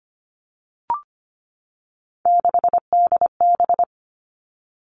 Morse code, heard as a keyed tone, about 25 wpm, 700 Hz. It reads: 6DB
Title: Three Challenging Letter-Number Combinations 25wpm